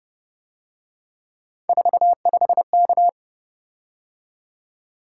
Morse code, heard as a keyed tone, about 28 wpm, 700 Hz. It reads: 45K
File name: Speed-Racing - Three Letter-Number Combinations - 25wpm